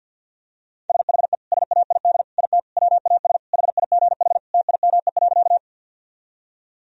Morse code, heard as a keyed tone, about 47 wpm, 700 Hz. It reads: SHE SAID IT WAS HIGH TIME.
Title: Sentences from Top 1000 Words - Review Part 1 - 50wpm